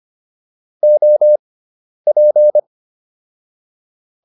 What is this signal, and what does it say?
Morse code, heard as a keyed tone, about 25 wpm, spaced as for 10 wpm, 600 Hz. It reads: OP